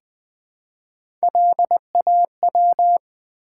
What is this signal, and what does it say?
Morse code, heard as a keyed tone, about 20 wpm, 700 Hz. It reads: LAW
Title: Build A Word - 1 Letter At A Time - 3 to 5-Letter Words - 20wpm